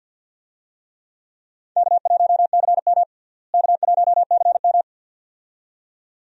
Morse code, heard as a keyed tone, about 47 wpm, 700 Hz. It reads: K1CM K1CM
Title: USA Call Signs - 4 Characters - Repeated 2x - 50wpm